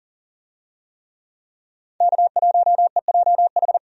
Morse code, heard as a keyed tone, about 40 wpm, 700 Hz. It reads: K1EJH
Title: USA Call Signs - 5 Characters - 40wpm